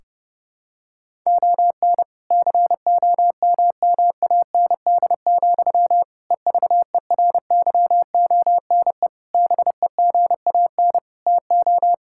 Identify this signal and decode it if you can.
Morse code, heard as a keyed tone, about 30 wpm, 700 Hz. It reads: ON COMMAND, EVERYONE BEGAN TO
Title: Sentences from Top 1600 Words 30wpm